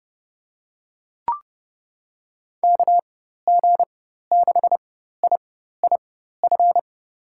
Morse code, heard as a keyed tone, about 30 wpm, spaced as for 14 wpm, 700 Hz. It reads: KG6IIF
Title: USA Call Signs - 6 Characters - 4x Character Spacing - 30wpm